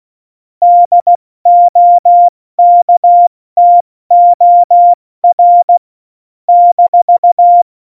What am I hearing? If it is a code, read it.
Morse code, heard as a keyed tone, about 16 wpm, 700 Hz. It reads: DOKTOR -
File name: Morsekurs / Witze / Witz 20. Februar 2018, 70 F80